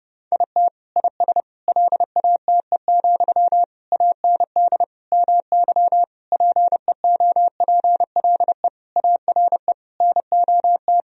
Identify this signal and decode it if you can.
Morse code, heard as a keyed tone, about 30 wpm, 700 Hz. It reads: IT IS LATE, AND MY PEOPLE ARE NOT
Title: Sentences from Top 1000 Words - Review Part 3 - 30wpm